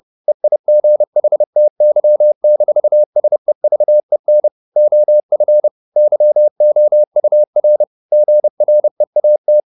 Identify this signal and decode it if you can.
Morse code, heard as a keyed tone, about 30 wpm, 600 Hz. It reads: EIGHTY-SEVEN OF YOUR GREAT
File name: the old man and the sea / the old man and the sea 1 30wpm